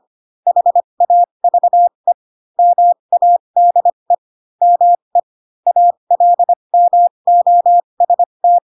Morse code, heard as a keyed tone, about 25 wpm, spaced as for 22 wpm, 700 Hz. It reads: HAVE MADE ME ALMOST
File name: Pinocchio / Pinocchio ch2 18 25 ik1hge